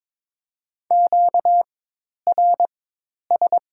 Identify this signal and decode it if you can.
Morse code, heard as a keyed tone, about 22 wpm, 700 Hz. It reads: Q R S
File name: Top 90 QSO Elements - 4x Character Spacing - 22wpm